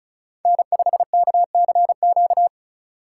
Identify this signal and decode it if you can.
Morse code, heard as a keyed tone, about 34 wpm, 700 Hz. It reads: N5KCQ
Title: USA Call Signs - 5 Characters - 3x Character Spacing - 35wpm